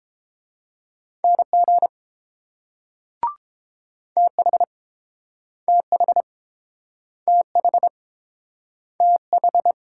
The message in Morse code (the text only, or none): NG TH TH TH TH